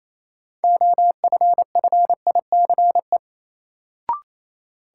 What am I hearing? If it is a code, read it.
Morse code, heard as a keyed tone, about 28 wpm, 700 Hz. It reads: OFFICE
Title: Sentences from Top 1000 Words - New Word Drill - 28wpm